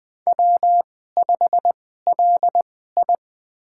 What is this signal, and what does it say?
Morse code, heard as a keyed tone, about 21 wpm, spaced as for 15 wpm, 700 Hz. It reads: W5LI
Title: USA Call Signs - 4 Characters - 2x Character Spacing - 20wpm